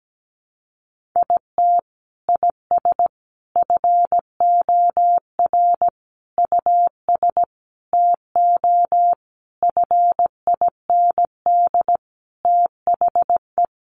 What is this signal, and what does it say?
Morse code, heard as a keyed tone, about 17 wpm, 700 Hz. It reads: IT IS FOR US TO FIND THE
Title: Sentences from Top 1300 Words 17wpm